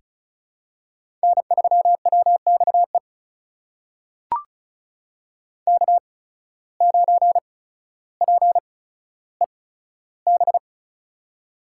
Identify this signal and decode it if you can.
Morse code, heard as a keyed tone, about 36 wpm, spaced as for 15 wpm, 700 Hz. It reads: N3WXE K9PEB
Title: USA Call Signs - 5 Characters - 8x Character Spacing - 35wpm